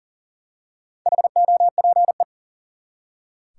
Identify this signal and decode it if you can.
Morse code, heard as a keyed tone, about 40 wpm, 700 Hz. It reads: HOPE
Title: Four Letter Words - 40wpm